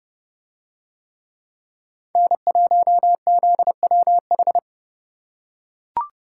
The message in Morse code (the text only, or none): N1ZWH